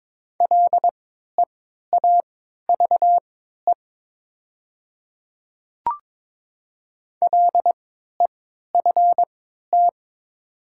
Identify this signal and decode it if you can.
Morse code, heard as a keyed tone, about 22 wpm, spaced as for 12 wpm, 700 Hz. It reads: LEAVE LEFT
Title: 3x Character Spacing - Top 500 Words - 22wpm